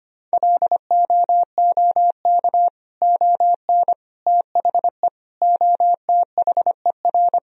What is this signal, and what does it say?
Morse code, heard as a keyed tone, about 25 wpm, 700 Hz. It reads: LOOK ON THE OTHER
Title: Sentences from Top 1000 Words - Review Part 1 - 25wpm